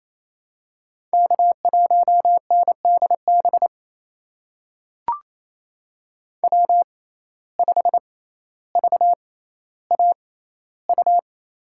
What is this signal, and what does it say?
Morse code, heard as a keyed tone, about 28 wpm, spaced as for 14 wpm, 700 Hz. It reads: K1NDB W5VAU
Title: USA Call Signs - 6x Character Spacing - 28wpm